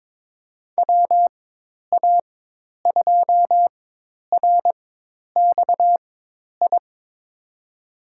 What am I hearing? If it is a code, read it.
Morse code, heard as a keyed tone, about 22 wpm, spaced as for 10 wpm, 700 Hz. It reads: WA2RXI
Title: USA Call Signs - 6 Characters - 4x Character Spacing - 22wpm